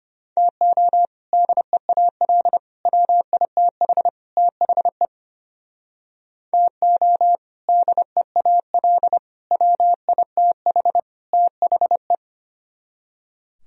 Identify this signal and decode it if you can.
Morse code, heard as a keyed tone, about 29 wpm, 700 Hz. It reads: TO DEAL WITH THE TO DEAL WITH THE
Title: Speed-Racing - Sets of 4 Words 25wpm